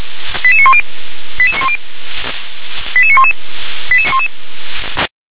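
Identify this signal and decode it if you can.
Five-tone selective call (ZVEI), heard as quick runs of five tones, tones of about 70 ms. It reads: 88910 88910 88911
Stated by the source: ZVEI five-tone selective call received by SDR.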